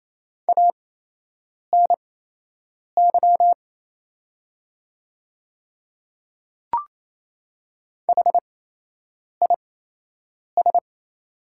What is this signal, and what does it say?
Morse code, heard as a keyed tone, about 28 wpm, spaced as for 8 wpm, 700 Hz. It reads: ANY HIS
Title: 8x Character Spacing - Top 200 Words - 28wpm